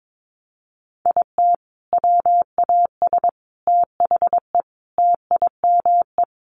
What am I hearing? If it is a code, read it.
Morse code, heard as a keyed tone, about 22 wpm, 700 Hz. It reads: IT WAS THE TIME
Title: Sentences from Top 700 Words 22wpm